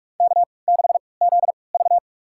Morse code, heard as a keyed tone, about 46 wpm, spaced as for 26 wpm, 700 Hz. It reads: K6ZV
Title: USA Call Signs - 3x Character Spacing - 45wpm